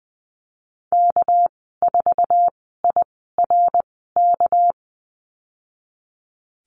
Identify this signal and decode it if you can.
Morse code, heard as a keyed tone, about 19 wpm, spaced as for 14 wpm, 700 Hz. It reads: K4IRK
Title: 200 Lesson 32g - Letter k - Call Signs 20wpm